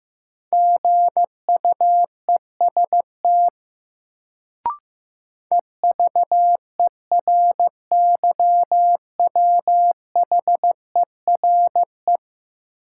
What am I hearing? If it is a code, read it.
Morse code, heard as a keyed tone, about 16 wpm, 700 Hz. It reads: GUEST EVERYWHERE
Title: Sentences from Top 1600 Words - New Word Drill - 15wpm